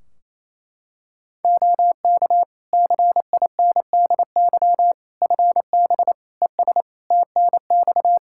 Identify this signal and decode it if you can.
Morse code, heard as a keyed tone, about 28 wpm, 700 Hz. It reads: OK CINDY FB ES TNX
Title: Practice QSO - Exchange 2 - Info - 28wpm